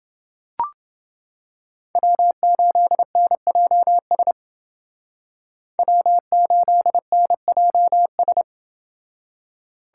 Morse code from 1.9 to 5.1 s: W8NJS

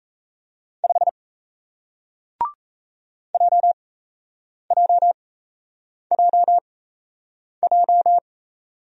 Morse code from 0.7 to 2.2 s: 5

